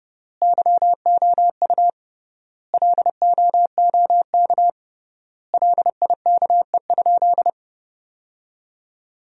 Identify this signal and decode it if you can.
Morse code, heard as a keyed tone, about 30 wpm, 700 Hz. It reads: YOU LOOK LIKE?
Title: 3x Word-Spacing - Sentences from Top 100 Words - 30wpm